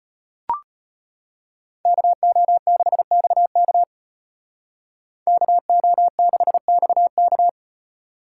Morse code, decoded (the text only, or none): KO6XK KO6XK